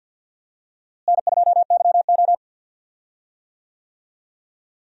Morse code, heard as a keyed tone, about 47 wpm, 700 Hz. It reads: N2YO
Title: USA Call Signs - 4 Characters - Repeated 3x - 50wpm